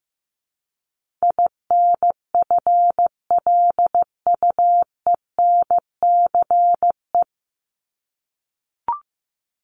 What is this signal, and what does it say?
Morse code, heard as a keyed tone, about 15 wpm, 700 Hz. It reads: INFLUENCE
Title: Top 1001-1250 Words 15wpm